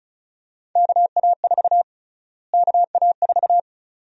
Morse code, heard as a keyed tone, about 36 wpm, 700 Hz. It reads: KA4 KA4
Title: USA Call Sign Prefixes - Repeated 2x - 35wpm